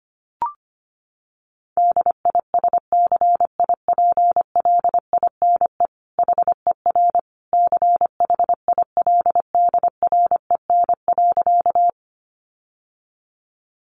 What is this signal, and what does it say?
Morse code, heard as a keyed tone, about 25 wpm, 700 Hz. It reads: DISCIPLINE HER CHILDREN.
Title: Sentences from Top 900 Words 25wpm